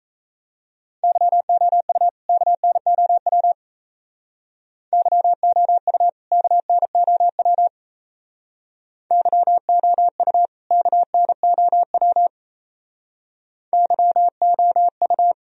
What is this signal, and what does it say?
Morse code, heard as a keyed tone, about 34 wpm, 700 Hz. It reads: YOU KNOW YOU KNOW YOU KNOW YOU
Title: Speed-Racing - Sets of 2 Words 28wpm